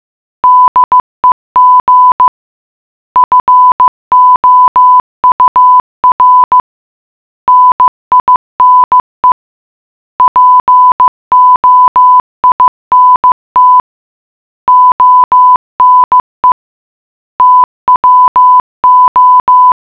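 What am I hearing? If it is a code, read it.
Morse code, heard as a keyed tone, about 15 wpm, 1000 Hz. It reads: DEG FOUR NINE POINT ONE TWO
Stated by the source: Morse puzzle for the geocache Lambertsetersenderen - Gacrux#8 (GCRHX0) in Oslo, Norway.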